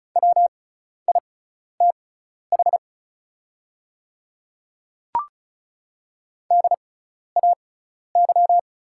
Morse code, heard as a keyed tone, about 36 wpm, spaced as for 12 wpm, 700 Hz. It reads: WITH DAY
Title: 6x Character Spacing - Top 100 Words - 35wpm